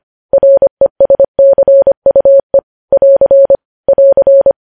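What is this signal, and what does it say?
Morse code, heard as a keyed tone, about 25 wpm, 550 Hz. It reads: RESCUE <AR> <AR>
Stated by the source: News headlines sent in Morse at 25 wpm.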